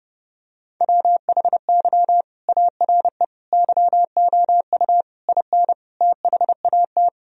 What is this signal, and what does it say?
Morse code, heard as a keyed tone, about 30 wpm, 700 Hz. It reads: WHY ARE YOU IN THAT
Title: Sentences from Top 1000 Words - Review Part 2 - 30wpm